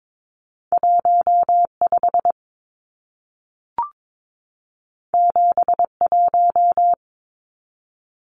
Morse code, heard as a keyed tone, about 22 wpm, 700 Hz. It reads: 15 71
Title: Numbers - 2-Digit 22wpm